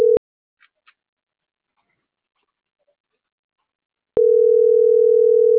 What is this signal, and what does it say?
Ringback tone (North America), 440 + 480 Hz.